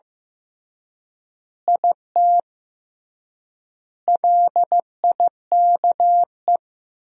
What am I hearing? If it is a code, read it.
Morse code, heard as a keyed tone, about 15 wpm, 700 Hz. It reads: IT LIKE
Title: 3x Word-Spacing - Sentences from Top 300 Words - 15wpm